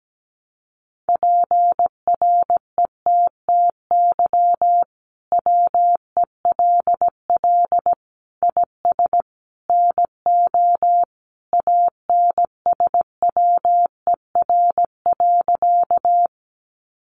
Morse code, heard as a keyed tone, about 17 wpm, 700 Hz. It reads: PRETTY WELL IS NO ANSWER.
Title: Sentences from Top 1000 Words 17wpm